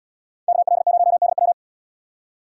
Morse code, heard as a keyed tone, about 78 wpm, 700 Hz. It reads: KL0DW